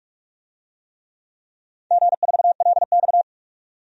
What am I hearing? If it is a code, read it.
Morse code, heard as a keyed tone, about 43 wpm, 700 Hz. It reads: G4LX
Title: DXCC-100 Call Signs - 4 Characters - 45wpm